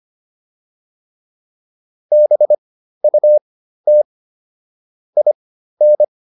Morse code, heard as a keyed tone, about 25 wpm, spaced as for 13 wpm, 600 Hz. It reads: BUT IN